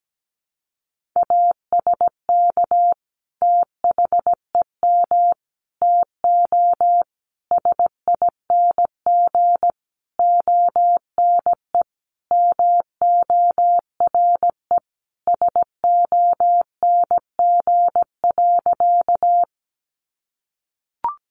Morse code, heard as a keyed tone, about 17 wpm, 700 Hz. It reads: ASK THEM TO SING ONE MORE SONG.
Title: Sentences from Top 400 Words 17wpm